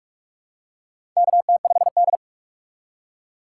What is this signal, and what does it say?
Morse code, heard as a keyed tone, about 46 wpm, 700 Hz. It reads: KT5D
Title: USA Call Signs - 4 Characters - 2x Character Spacing - 45wpm